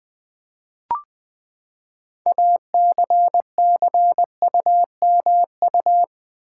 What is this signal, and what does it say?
Morse code, heard as a keyed tone, about 20 wpm, 700 Hz. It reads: ACCUMU
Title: Build A Word - 6 Letters At A Time - 12-Letter Words - 20wpm